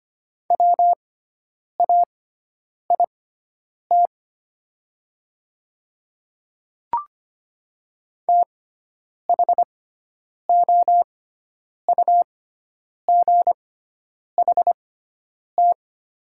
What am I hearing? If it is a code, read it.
Morse code, heard as a keyed tone, about 25 wpm, spaced as for 9 wpm, 700 Hz. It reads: WAIT THOUGHT
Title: 6x Character Spacing - Top 500 Words - 25wpm